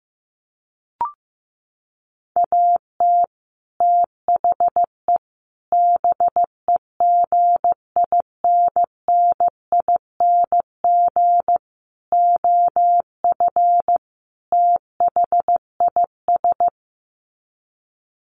Morse code, heard as a keyed tone, about 14 wpm, 700 Hz. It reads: AT THE BEGINNING OF THIS
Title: Sets of 5 Words Encore 15wpm